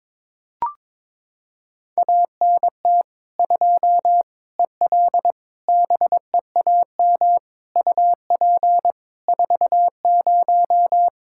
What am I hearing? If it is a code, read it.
Morse code, heard as a keyed tone, about 23 wpm, 700 Hz. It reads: ANT 2 EL BEAM UP 40
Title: Practice QSO - Exchange 2 - Info - 22wpm